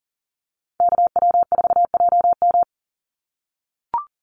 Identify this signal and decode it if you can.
Morse code, heard as a keyed tone, about 40 wpm, 700 Hz. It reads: KW4JM
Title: NA SOTA Chaser Callsigns 40wpm